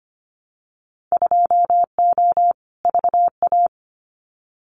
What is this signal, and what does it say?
Morse code, heard as a keyed tone, about 25 wpm, 690 Hz. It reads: 2O VA
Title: Winter Field Day Exchanges 25wpm